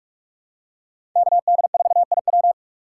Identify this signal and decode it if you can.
Morse code, heard as a keyed tone, about 44 wpm, 700 Hz. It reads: KD4IW
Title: USA Call Signs - 5 Characters - 6x Character Spacing - 45wpm